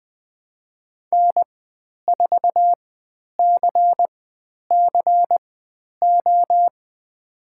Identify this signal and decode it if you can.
Morse code, heard as a keyed tone, about 20 wpm, spaced as for 10 wpm, 700 Hz. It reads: N4CCO
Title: 095 Lesson 16g - Letter c - Call Signs 20wpm